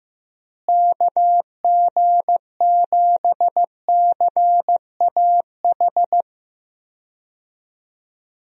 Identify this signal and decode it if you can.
Morse code, heard as a keyed tone, about 15 wpm, 700 Hz. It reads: KG7CAH
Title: USA Call Signs - 6 Characters - Repeated 2x - 15wpm